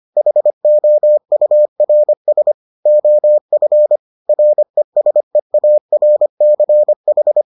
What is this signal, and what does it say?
Morse code, heard as a keyed tone, about 25 wpm, 600 Hz. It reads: HOURS OF RESEARCH